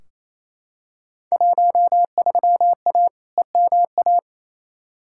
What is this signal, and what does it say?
Morse code, heard as a keyed tone, about 28 wpm, 700 Hz. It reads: 13A EMA
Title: Field Day Exchanges Encore - 28wpm